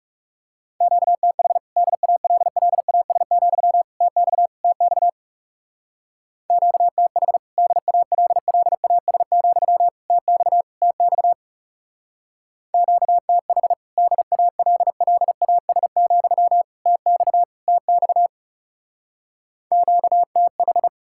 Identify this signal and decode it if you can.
Morse code, heard as a keyed tone, about 36 wpm, 700 Hz. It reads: QTH DALLAS, TX TX QTH DALLAS, TX TX QTH DALLAS, TX TX QTH
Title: Speed-Racing Common QSO Phrases - 30wpm